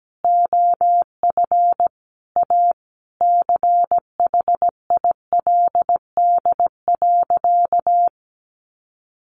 Morse code, heard as a keyed tone, about 17 wpm, 700 Hz. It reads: OF A CHILD.
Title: Sentences from Top 1400 Words 17wpm